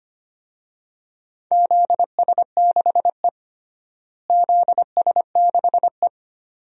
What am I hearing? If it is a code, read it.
Morse code, heard as a keyed tone, about 25 wpm, 700 Hz. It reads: ZS6E ZS6E
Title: DXCC-100 Call Signs - 4 Characters - Repeated 2x - 25wpm